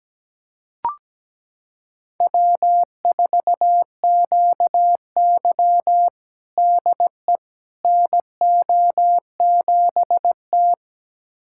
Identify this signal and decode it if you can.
Morse code, heard as a keyed tone, about 17 wpm, 700 Hz. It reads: W4QY DE NO7T
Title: USA Callsign DE Callsign - 4 Characters - 17wpm